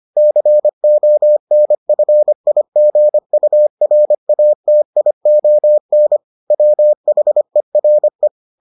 Morse code, heard as a keyed tone, about 25 wpm, 600 Hz. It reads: CONFIGURATION WHERE